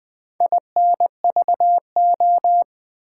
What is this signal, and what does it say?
Morse code, heard as a keyed tone, about 20 wpm, 700 Hz. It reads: INVO
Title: Build A Word - 2 Letters At A Time - 8-Letter Words - 20wpm